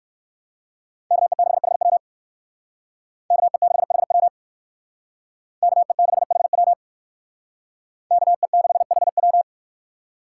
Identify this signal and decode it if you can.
Morse code, heard as a keyed tone, about 57 wpm, 690 Hz. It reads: KE6HW KE6HW KE6HW KE6HW